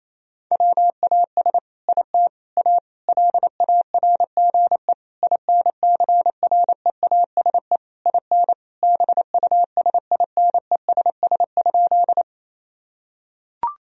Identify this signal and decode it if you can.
Morse code, heard as a keyed tone, about 28 wpm, 700 Hz. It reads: WAS IT A LARGE INCREASE IN BUSINESS?
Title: Sentences from Top 1000 Words - Review Part 4 - 28wpm